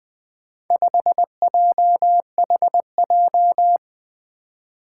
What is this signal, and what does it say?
Morse code, heard as a keyed tone, about 20 wpm, 700 Hz. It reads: 5JHJ
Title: Four Challenging Letter-Number Combinations 20wpm